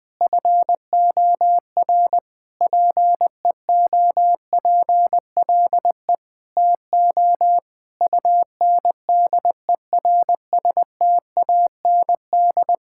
Morse code, heard as a keyed tone, about 20 wpm, 700 Hz. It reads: FOR PEOPLE TO UNDERSTAND
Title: Sentences with 7-Word Phrases - Spoken Beginning - 20wpm